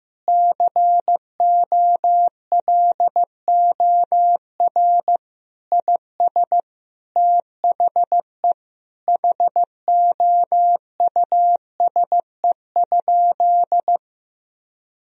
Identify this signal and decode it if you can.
Morse code, heard as a keyed tone, about 16 wpm, 700 Hz. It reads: COLOR IS THE HOUSE?
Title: Sentences from Top 1000 Words - Review Part 2 - 15wpm